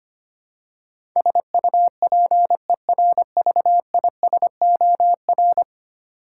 Morse code, heard as a keyed tone, about 25 wpm, 700 Hz. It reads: SUPERVISOR
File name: Ten Letter Words - 25wpm